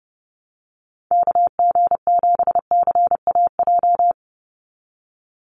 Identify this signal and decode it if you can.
Morse code, heard as a keyed tone, about 30 wpm, 700 Hz. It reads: KG7CAJ
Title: USA Call Signs 30wpm